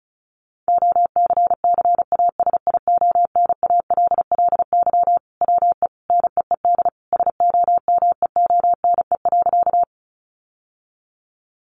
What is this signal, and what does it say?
Morse code, heard as a keyed tone, about 34 wpm, 700 Hz. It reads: OCCASIONALLY WE NEED SOMEONE.
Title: Sentences from Top 1300 Words 35wpm